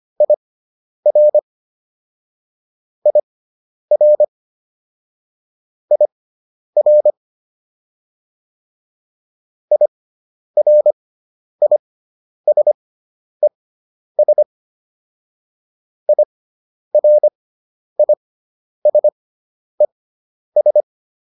Morse code, heard as a keyed tone, about 25 wpm, spaced as for 10 wpm, 600 Hz. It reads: IR IR IR IRISES IRISES